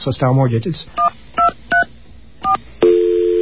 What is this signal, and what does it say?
DTMF, heard as telephone touch tones, tones of about 130 ms, with speech. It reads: 123 4